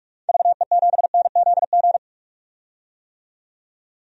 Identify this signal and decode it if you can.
Morse code, heard as a keyed tone, about 46 wpm, 700 Hz. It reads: VE7NZG
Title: DXCC-100 Call Signs - 6 Characters - 45wpm